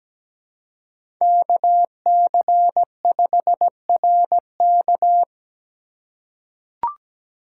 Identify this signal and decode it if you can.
Morse code, heard as a keyed tone, about 17 wpm, 700 Hz. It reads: KC5RK
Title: USA Call Signs - 5 Characters - 4x Character Spacing - 17wpm